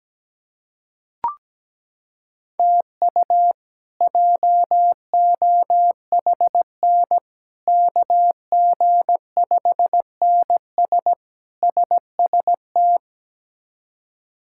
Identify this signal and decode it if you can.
Morse code, heard as a keyed tone, about 17 wpm, 700 Hz. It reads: TU JOHN KG5NS SST
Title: K1USN SST QSO Practice - 17wpm